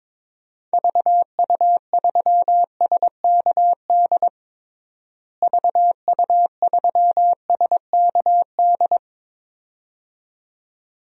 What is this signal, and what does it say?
Morse code, heard as a keyed tone, about 22 wpm, 700 Hz. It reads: VU3SKD VU3SKD
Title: DXCC-100 Call Signs - 6 Characters - Repeated 2x - 22wpm